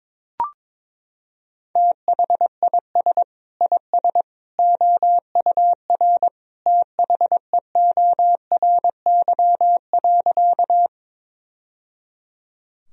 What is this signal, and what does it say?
Morse code, heard as a keyed tone, about 22 wpm, 700 Hz. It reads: THIS IS OUR THEORY.
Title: Sentences from Top 1000 Words - Review Part 3 - 22wpm